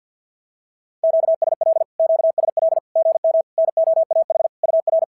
Morse code, heard as a keyed tone, about 54 wpm, 650 Hz. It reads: QSL QSL GM NOAH UR
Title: SOTA QSO Practice 50wpm